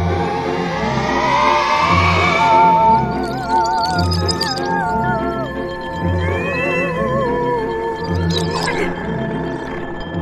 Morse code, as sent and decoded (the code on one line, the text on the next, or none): none